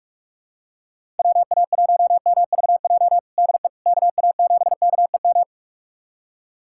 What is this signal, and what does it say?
Morse code, heard as a keyed tone, about 43 wpm, 700 Hz. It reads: WA1MVJ DE KA7KEM
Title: USA Callsign DE Callsign - 6 Characters - 45wpm